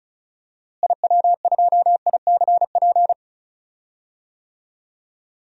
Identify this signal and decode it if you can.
Morse code, heard as a keyed tone, about 36 wpm, 700 Hz. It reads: IW2ICP
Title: DXCC-100 Call Signs - 6 Characters - Repeated 2x - 35wpm